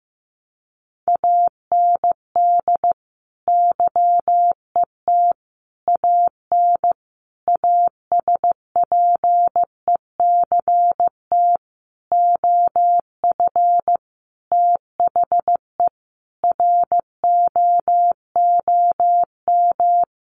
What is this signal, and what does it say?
Morse code, heard as a keyed tone, about 15 wpm, 700 Hz. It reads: AND YET AN ASPECT OF THE ROOM
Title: Sentences from Top 1300 Words 15wpm